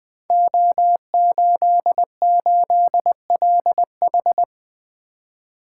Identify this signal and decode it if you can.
Morse code, heard as a keyed tone, about 20 wpm, 700 Hz. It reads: O88LH
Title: Code Groups - 5 Letter:Numbers - 20wpm